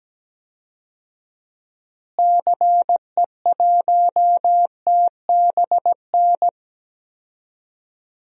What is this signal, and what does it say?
Morse code, heard as a keyed tone, about 17 wpm, 700 Hz. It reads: CE1TBN
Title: DXCC-100 Call Signs - 6 Characters - Repeated 3x - 17wpm